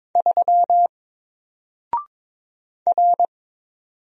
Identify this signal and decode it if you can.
Morse code, heard as a keyed tone, about 22 wpm, 700 Hz. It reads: 3 R
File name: Single-Letter-Number-22wpm